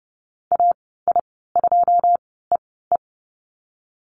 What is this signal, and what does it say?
Morse code, heard as a keyed tone, about 30 wpm, spaced as for 17 wpm, 700 Hz. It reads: AI2EE
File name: Lesson 15g - Number 5 - Call Signs 30wpm